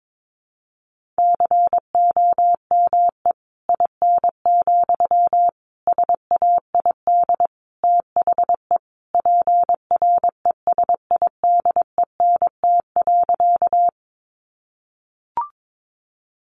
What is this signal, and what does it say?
Morse code, heard as a keyed tone, about 22 wpm, 700 Hz. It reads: COME IN, SAID THE PRESIDENT.
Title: Sentences from Top 1000 Words 22wpm